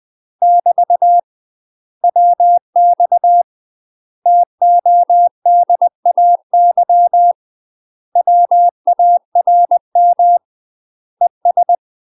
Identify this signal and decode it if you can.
Morse code, heard as a keyed tone, about 20 wpm, 700 Hz. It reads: <BT> WX TODAY WARM ES
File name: UK MASTER QSOs TRAINING / 20WPM / UK MASTER QSOs 0062